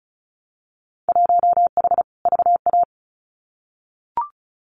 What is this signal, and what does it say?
Morse code, heard as a keyed tone, about 36 wpm, 700 Hz. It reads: 1H VA
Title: Winter Field Day Exchanges 35wpm